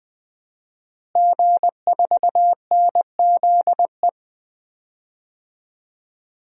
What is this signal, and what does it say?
Morse code, heard as a keyed tone, about 20 wpm, 700 Hz. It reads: G4NZE
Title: DXCC-100 Call Signs - 5 Characters - Repeated 2x - 20wpm